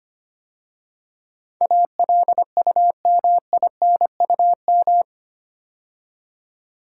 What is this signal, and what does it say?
Morse code, heard as a keyed tone, about 25 wpm, 700 Hz. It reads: ALUMINUM
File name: 8 Letter Words - First 3 Letters Given - 25wpm